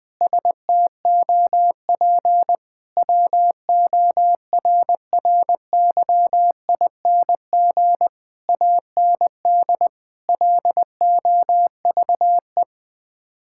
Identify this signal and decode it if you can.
Morse code, heard as a keyed tone, about 21 wpm, 700 Hz. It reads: STOP WORRYING AND LOVE
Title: Sets of 8 Words - 20wpm